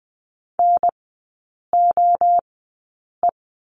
Morse code, heard as a keyed tone, about 19 wpm, spaced as for 8 wpm, 700 Hz. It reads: NOE
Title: 041 Lesson 8e - Number 1 - Three Characters 20wpm